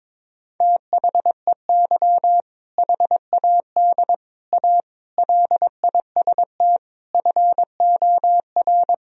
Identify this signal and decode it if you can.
Morse code, heard as a keyed tone, about 22 wpm, 700 Hz. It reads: THEY HAD A LIST FOR
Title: Sentences from Top 1000 Words - Review Part 2 - 22wpm